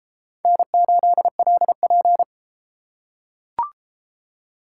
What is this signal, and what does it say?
Morse code, heard as a keyed tone, about 30 wpm, 700 Hz. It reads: N8LP